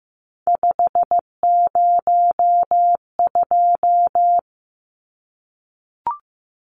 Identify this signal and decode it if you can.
Morse code, heard as a keyed tone, about 15 wpm, 700 Hz. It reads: 502
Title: Numbers - 3-Digit 15wpm